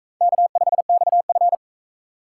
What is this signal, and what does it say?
Morse code, heard as a keyed tone, about 38 wpm, 700 Hz. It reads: K5XF